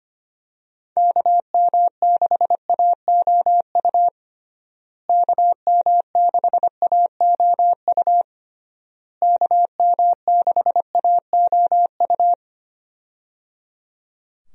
Morse code, heard as a keyed tone, about 25 wpm, 700 Hz. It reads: KM6AOU KM6AOU KM6AOU
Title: USA Call Signs - 6 Characters - Repeated 3x - 25wpm